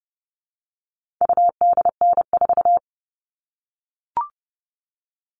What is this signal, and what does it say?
Morse code, heard as a keyed tone, about 30 wpm, 700 Hz. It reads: UDN4
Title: All Cut Numbers - 4-Digit 30wpm